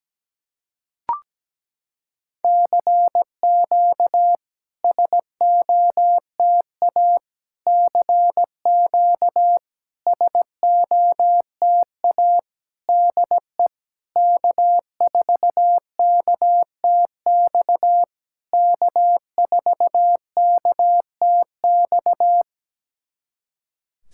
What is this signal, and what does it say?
Morse code, heard as a keyed tone, about 17 wpm, 700 Hz. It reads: CQ SOTA CQ SOTA DE K4KTX K4KTX
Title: SOTA QSO Practice 17wpm